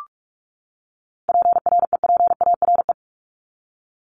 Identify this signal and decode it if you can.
Morse code, heard as a keyed tone, about 44 wpm, 700 Hz. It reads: PREPARE
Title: Top 751-1000 Words 45wpm